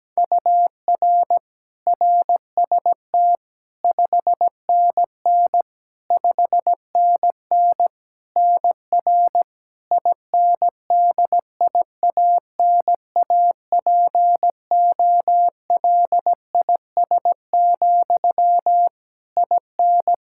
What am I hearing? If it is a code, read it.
Morse code, heard as a keyed tone, about 17 wpm, 700 Hz. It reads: UR RST 5NN 5NN NR INDIANAPOLIS, IN
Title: K1USN SST QSO Practice 17wpm